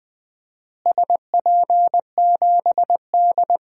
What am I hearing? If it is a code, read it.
Morse code, heard as a keyed tone, about 21 wpm, 700 Hz. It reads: SP7D
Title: DXCC-100 Call Signs - 4 Characters - Repeated 2x - 20wpm